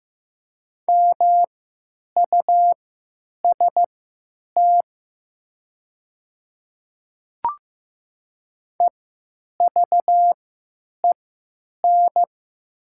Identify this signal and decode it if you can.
Morse code, heard as a keyed tone, about 15 wpm, spaced as for 9 wpm, 700 Hz. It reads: MUST EVEN
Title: 3x Character Spacing - Top 300 Words - 15wpm